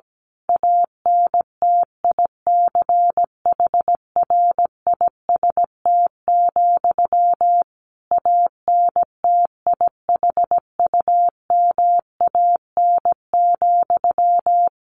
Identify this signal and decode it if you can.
Morse code, heard as a keyed tone, about 17 wpm, 700 Hz. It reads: ANTICHRIST, ANTIHUMAN,
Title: A Sound Of Thunder 17wpm